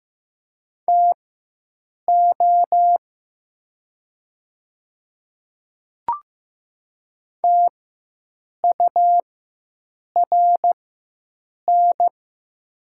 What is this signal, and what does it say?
Morse code, heard as a keyed tone, about 15 wpm, spaced as for 7 wpm, 700 Hz. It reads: TO TURN
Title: 4x Character Spacing - Top 200 Words - 15wpm